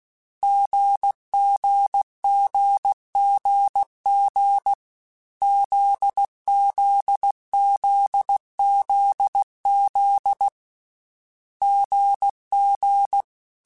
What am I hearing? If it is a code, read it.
Morse code, heard as a keyed tone, about 16 wpm, 800 Hz. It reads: GGGGG ZZZZZ GG